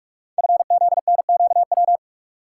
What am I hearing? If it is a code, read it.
Morse code, heard as a keyed tone, about 45 wpm, 690 Hz. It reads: FZNQW
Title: Code Groups - 5 Letters - 45wpm